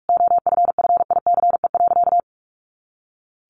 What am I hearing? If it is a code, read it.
Morse code, heard as a keyed tone, about 44 wpm, 700 Hz. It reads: OFFICE.
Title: Additional Easy Phrases 45wpm